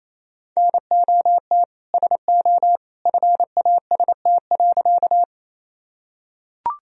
Morse code, heard as a keyed tone, about 28 wpm, 700 Hz. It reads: NOT SO FAST.
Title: Sentences from Top 1000 Words - Review Part 2 - 28wpm